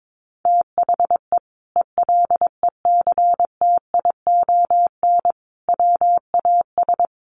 Morse code, heard as a keyed tone, about 22 wpm, 700 Hz. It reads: THE ELECTION WAS
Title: Sentences from Top 1500 Words 22wpm